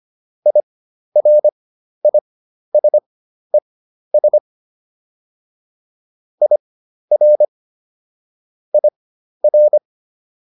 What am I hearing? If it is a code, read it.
Morse code, heard as a keyed tone, about 25 wpm, spaced as for 12 wpm, 600 Hz. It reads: IRISES IR IR